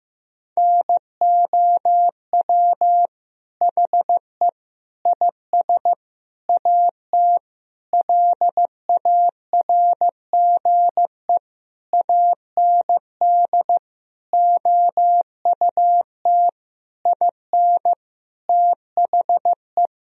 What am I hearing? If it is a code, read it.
Morse code, heard as a keyed tone, about 15 wpm, 700 Hz. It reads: NOW HE IS AT LARGE AND OUT IN THE
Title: Sentences from Top 1000 Words - Review Part 1 - 15wpm